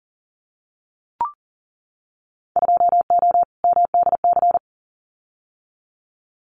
Morse code, heard as a keyed tone, about 42 wpm, 700 Hz. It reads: 2O MDC
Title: Winter Field Day Exchanges 40wpm